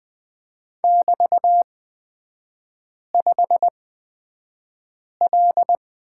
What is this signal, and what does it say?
Morse code, heard as a keyed tone, about 20 wpm, spaced as for 5 wpm, 700 Hz. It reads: <BT>5L
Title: 255 Lesson 41e - Period - Three Characters 20wpm